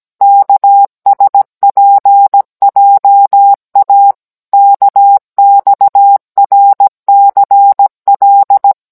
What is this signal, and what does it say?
Morse code, heard as a keyed tone, about 17 wpm, 800 Hz. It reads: KSPJA KXRCL